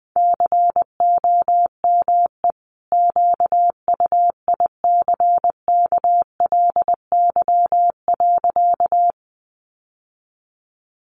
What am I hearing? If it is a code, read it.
Morse code, heard as a keyed tone, about 20 wpm, 700 Hz. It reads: COME QUICKLY.
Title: Sentences from Top 700 Words 20wpm